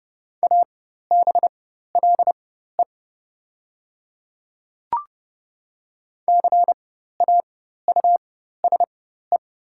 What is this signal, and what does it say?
Morse code, heard as a keyed tone, about 30 wpm, spaced as for 14 wpm, 690 Hz. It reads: ABLE CAUSE
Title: 4x Character Spacing - Top 500 Words - 30wpm